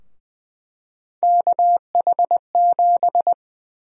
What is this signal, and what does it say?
Morse code, heard as a keyed tone, about 20 wpm, 700 Hz. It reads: KH7
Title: USA Call Sign Prefixes - 20wpm